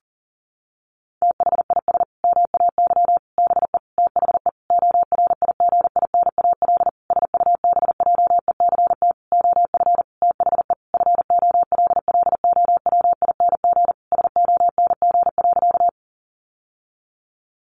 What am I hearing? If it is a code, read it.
Morse code, heard as a keyed tone, about 40 wpm, 700 Hz. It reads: THIS MAY BE THE ORIGINAL SUBJECT OF THE FOLLOWING SONG.
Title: Sentences from Top 900 Words 40wpm